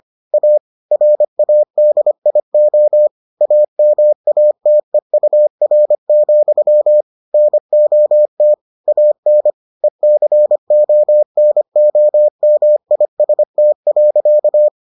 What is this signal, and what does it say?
Morse code, heard as a keyed tone, about 25 wpm, 600 Hz. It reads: A RADIO AMATEUR, NOT AN ECONOMIST.